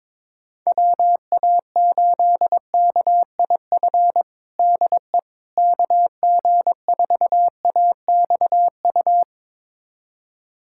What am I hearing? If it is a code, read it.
Morse code, heard as a keyed tone, about 22 wpm, 700 Hz. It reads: WA8KIF DE KG4AXU
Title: USA Callsign DE Callsign - 6 Characters - 22wpm